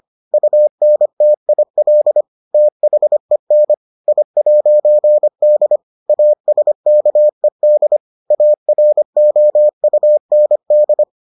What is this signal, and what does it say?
Morse code, heard as a keyed tone, about 25 wpm, 600 Hz. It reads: UNTIL THEN I'D ASKED AROUND